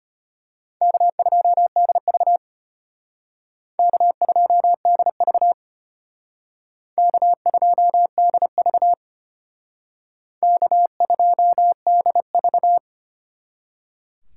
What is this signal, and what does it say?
Morse code, heard as a keyed tone, about 31 wpm, 700 Hz. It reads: K2DV K2DV K2DV K2DV